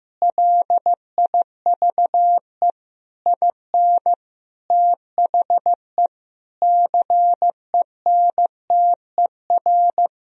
Morse code, heard as a keyed tone, about 15 wpm, 700 Hz. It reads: LIVE IN THE CENTER
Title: Sentences from Top 1000 Words - Review Part 2 - 15wpm